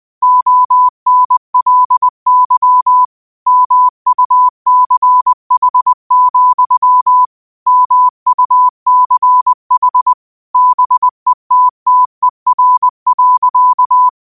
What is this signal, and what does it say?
Morse code, heard as a keyed tone, about 20 wpm, 1000 Hz. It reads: ONLY MUCH, MUCH BETTER.
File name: qotdfiles / 20wpm